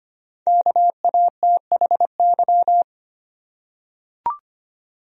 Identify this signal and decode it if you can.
Morse code, heard as a keyed tone, about 25 wpm, 700 Hz. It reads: KATHY
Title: USA Common Names - 25wpm